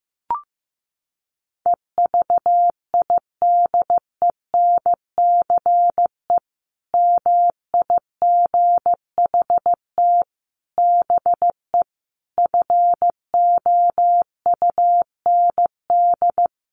Morse code, heard as a keyed tone, about 14 wpm, 700 Hz. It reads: EVIDENCE MIGHT BE FOUND
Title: Sentences from Top 1300 Words 15wpm